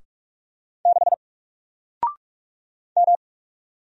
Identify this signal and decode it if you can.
Morse code, heard as a keyed tone, about 44 wpm, 700 Hz. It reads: 6 M
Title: Single Letter-Number Alt 45wpm